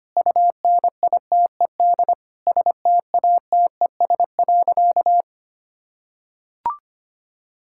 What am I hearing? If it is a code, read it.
Morse code, heard as a keyed tone, about 25 wpm, 700 Hz. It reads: UNITED STATES.
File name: practice / 25wpm / Sentences with 4-Word Phrases - 25wpm